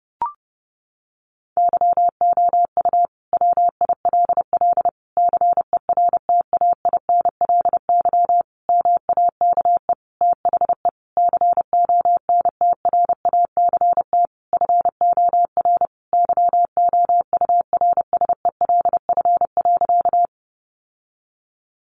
Morse code, heard as a keyed tone, about 30 wpm, 700 Hz. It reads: YOU WILL CERTAINLY MAKE THE CONTRACT FOR YOURSELF.
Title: Sentences from Top 1400 Words 30wpm